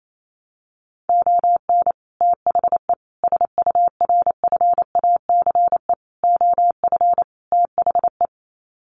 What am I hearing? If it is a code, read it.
Morse code, heard as a keyed tone, about 28 wpm, 700 Hz. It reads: ON THE SURFACE OF THE
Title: Sets of 5 Words Encore 28wpm